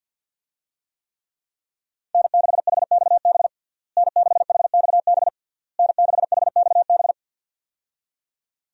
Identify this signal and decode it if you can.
Morse code, heard as a keyed tone, about 51 wpm, 700 Hz. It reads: N6HXB N6HXB N6HXB
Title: USA Call Signs - 5 Characters - Repeated 3x - 50wpm